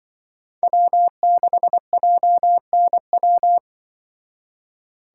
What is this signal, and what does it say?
Morse code, heard as a keyed tone, about 24 wpm, 700 Hz. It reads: W6JNW
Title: Speed Racing - USA 1x3 Call Signs 20wpm